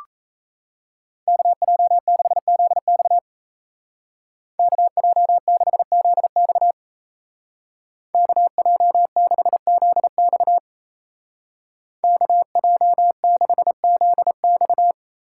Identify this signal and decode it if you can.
Morse code, heard as a keyed tone, about 33 wpm, 700 Hz. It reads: KJ6ZX KJ6ZX KJ6ZX KJ6ZX
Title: Speed Racing - USA 2x2 Call Signs 28wpm